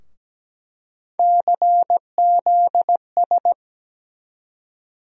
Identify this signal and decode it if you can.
Morse code, heard as a keyed tone, about 17 wpm, 700 Hz. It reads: CZS
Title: USA Call Sign Postfixes - 3 Letters 17wpm